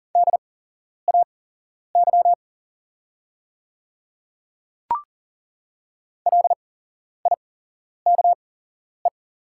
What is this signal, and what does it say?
Morse code, heard as a keyed tone, about 40 wpm, spaced as for 11 wpm, 700 Hz. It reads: DAY LIKE
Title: 8x Character Spacing - Top 200 Words - 40wpm